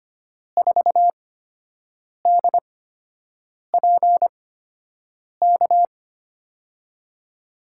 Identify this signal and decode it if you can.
Morse code, heard as a keyed tone, about 25 wpm, spaced as for 7 wpm, 700 Hz. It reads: 4DPK
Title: USA Call Signs - 6 Characters - 8x Character Spacing - 25wpm